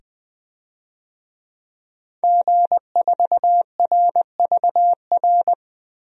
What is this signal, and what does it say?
Morse code, heard as a keyed tone, about 20 wpm, 700 Hz. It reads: G4RVR
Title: DXCC-100 Call Signs - 5 Characters - 20wpm